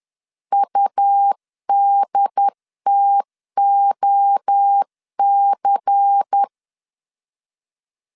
Morse code, heard as a keyed tone, about 10 wpm, 800 Hz. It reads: UDTOC
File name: CW / DET / 1 6 1500C S C 10wpm 4252